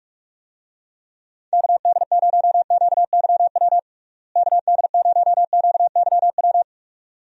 Morse code, heard as a keyed tone, about 45 wpm, 700 Hz. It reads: KD0QYW KD0QYW
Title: USA Call Signs - 6 Characters - Repeated 2x - 45wpm